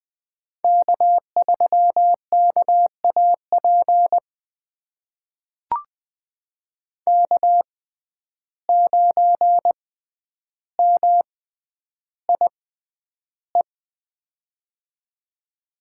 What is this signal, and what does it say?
Morse code, heard as a keyed tone, about 20 wpm, spaced as for 10 wpm, 700 Hz. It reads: K3KAP K9MIE
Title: USA Call Signs - 5 Characters - 6x Character Spacing - 20wpm